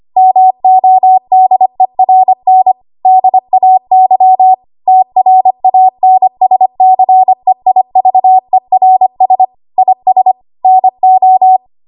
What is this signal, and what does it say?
Morse code, heard as a keyed tone, about 25 wpm, 750 Hz. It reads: MODERN DAY TRANSCEIVERS IS NO
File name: Archive / 25 WPM / 25WPM